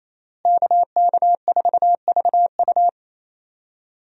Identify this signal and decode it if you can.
Morse code, heard as a keyed tone, about 28 wpm, 700 Hz. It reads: KK4VU
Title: USA Call Signs - 5 Characters - Repeated 2x - 28wpm